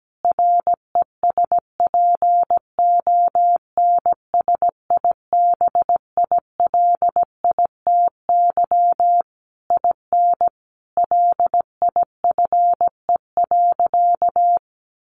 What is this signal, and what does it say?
Morse code, heard as a keyed tone, about 17 wpm, 700 Hz. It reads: RESPONSIBILITY IN LIFE.
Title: Sentences from Top 1200 Words 17wpm